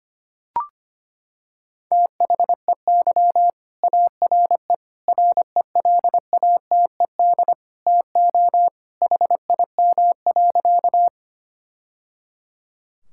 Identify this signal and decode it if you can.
Morse code, heard as a keyed tone, about 25 wpm, 700 Hz. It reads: THEY ARE RELATED TO HIM.
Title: Sentences from Top 1000 Words - Review Part 5 - 25wpm